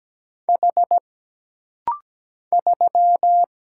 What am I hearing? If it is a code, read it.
Morse code, heard as a keyed tone, about 17 wpm, 700 Hz. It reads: H 3
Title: Single Letter-Number - Rapid-Fire 17wpm